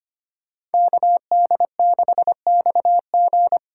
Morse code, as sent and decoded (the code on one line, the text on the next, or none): -.- -.. -.... -..- --.
KD6XG